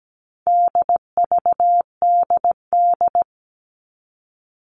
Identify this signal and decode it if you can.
Morse code, heard as a keyed tone, about 17 wpm, 700 Hz. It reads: DVDD
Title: All Cut Numbers - 4-Digit 17wpm